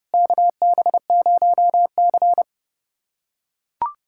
Morse code, heard as a keyed tone, about 30 wpm, 700 Hz. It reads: KB0C
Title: USA Callsign DE Callsign - 4 Characters - 30wpm